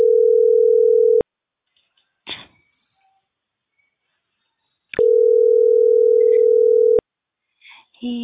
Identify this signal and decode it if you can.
Ringback tone (North America), 440 + 480 Hz.